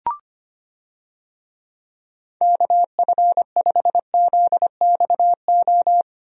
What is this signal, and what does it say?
Morse code, heard as a keyed tone, about 25 wpm, 700 Hz. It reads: KF5ZXO
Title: USA Call Signs - 6 Characters - Repeated 3x - 25wpm